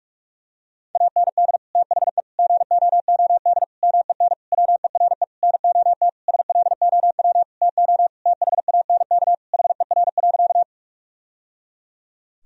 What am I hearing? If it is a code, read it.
Morse code, heard as a keyed tone, about 46 wpm, 700 Hz. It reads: AND THE GOOD MEN WERE NOT SLOW TO THANK HER.
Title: Sentences from Top 1000 Words - Review Part 2 - 45wpm